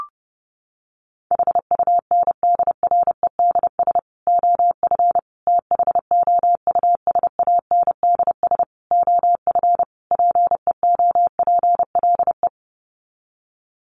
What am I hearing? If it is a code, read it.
Morse code, heard as a keyed tone, about 30 wpm, 700 Hz. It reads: HUNDREDS OF THOUSANDS OF PEOPLE
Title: Sets of 5 Words Encore2 30wpm